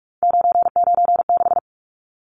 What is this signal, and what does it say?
Morse code, heard as a keyed tone, about 44 wpm, 700 Hz. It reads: 996
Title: Numbers - 3-Digit 45wpm